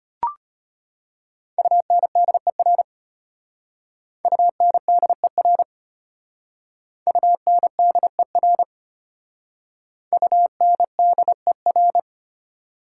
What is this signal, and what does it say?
Morse code, heard as a keyed tone, about 32 wpm, 700 Hz. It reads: UNDER UNDER UNDER UNDER